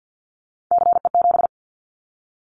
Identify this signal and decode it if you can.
Morse code, heard as a keyed tone, about 48 wpm, 700 Hz. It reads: CE7
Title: Callsign Prefixes for Top 100 DXCC - 50wpm